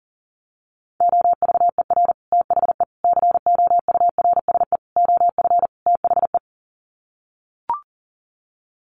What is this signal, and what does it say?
Morse code, heard as a keyed tone, about 40 wpm, 700 Hz. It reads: OVER THE COURSE OF THE
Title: Sets of 5 Words 40wpm